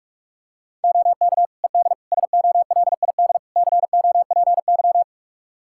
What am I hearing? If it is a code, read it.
Morse code, heard as a keyed tone, about 46 wpm, 700 Hz. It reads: OK ED SOLID COPY
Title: K1USN SST QSO Practice 45wpm